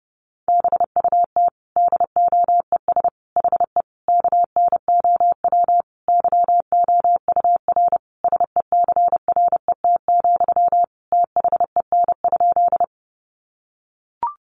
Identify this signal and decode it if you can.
Morse code, heard as a keyed tone, about 30 wpm, 700 Hz. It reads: BUT DOES HE KNOW YOUR SECRET, THEN?
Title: Sentences from Top 1200 Words 30wpm